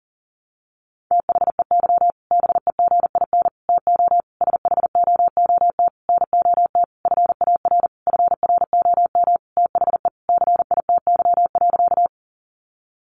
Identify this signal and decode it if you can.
Morse code, heard as a keyed tone, about 40 wpm, 700 Hz. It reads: THEY BEGIN TO SHOOT NOT FAR FROM THE CITY.
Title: Sentences from Top 1200 Words 40wpm